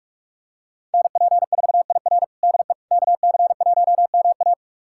Morse code, heard as a keyed tone, about 48 wpm, 700 Hz. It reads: NP4IR DE KC1MA
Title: USA Callsign DE Callsign - 5 Characters - 45wpm